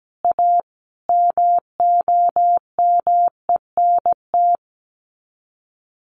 Morse code, heard as a keyed tone, about 17 wpm, 700 Hz. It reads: A MOMENT
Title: Sets of 5 Words 17wpm